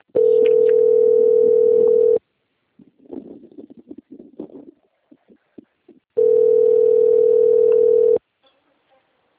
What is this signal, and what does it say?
Ringback tone (North America), 440 + 480 Hz.